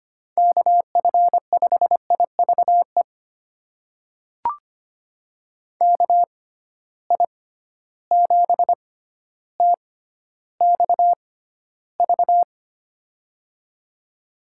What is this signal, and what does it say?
Morse code, heard as a keyed tone, about 25 wpm, spaced as for 13 wpm, 700 Hz. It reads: KF5IVE KI7TXV
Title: USA Call Signs - 6 Characters - 6x Character Spacing - 25wpm